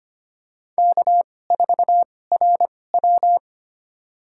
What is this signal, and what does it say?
Morse code, heard as a keyed tone, about 25 wpm, spaced as for 18 wpm, 700 Hz. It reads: K4RW
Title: USA Call Signs - 2x Character Spacing - 25wpm